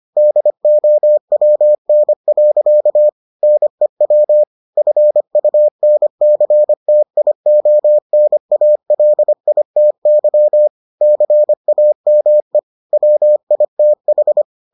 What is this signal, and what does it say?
Morse code, heard as a keyed tone, about 25 wpm, 600 Hz. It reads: DOWN. NEW FUNCTIONALITY CAME WITH